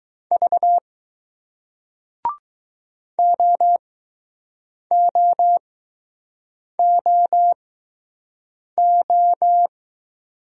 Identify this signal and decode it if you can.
Morse code, heard as a keyed tone, about 20 wpm, spaced as for 6 wpm, 700 Hz. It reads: V OOOO